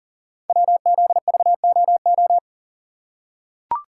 Morse code, heard as a keyed tone, about 40 wpm, 690 Hz. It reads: WZVOO
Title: Code Groups - 5 Letter:Numbers - 40wpm